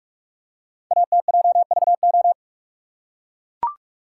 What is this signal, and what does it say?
Morse code, heard as a keyed tone, about 44 wpm, 690 Hz. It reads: ATJVO
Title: Code Groups - 5 Letters - 45wpm